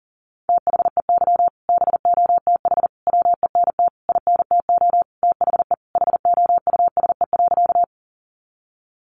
Morse code, heard as a keyed tone, about 40 wpm, 700 Hz. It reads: THEY BOTH WENT INTO THE HOUSE.
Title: Sentences from Top 600 Words 40wpm